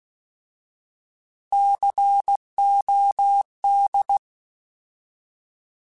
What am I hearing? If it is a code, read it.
Morse code, heard as a keyed tone, about 16 wpm, 800 Hz. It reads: COD
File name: Alphabetical / Alphabetical mp3s / Cs @15